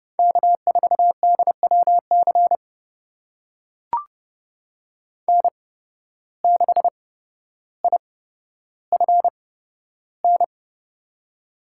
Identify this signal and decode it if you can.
Morse code, heard as a keyed tone, about 30 wpm, spaced as for 13 wpm, 700 Hz. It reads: K4DWC N6IFN
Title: USA Call Signs - 5 Characters - 8x Character Spacing - 30wpm